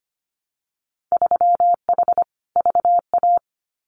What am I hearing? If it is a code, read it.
Morse code, heard as a keyed tone, about 25 wpm, 700 Hz. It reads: 3H VA
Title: Winter Field Day Exchanges 25wpm